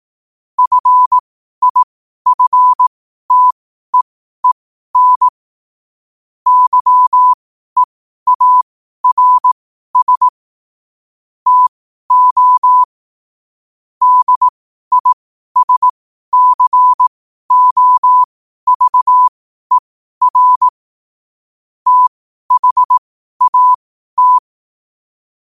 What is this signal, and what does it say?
Morse code, heard as a keyed tone, about 18 wpm, spaced as for 12 wpm, 1000 Hz. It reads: FIFTEEN YEARS TO DISCOVER THAT
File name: qotdfiles / 12wpm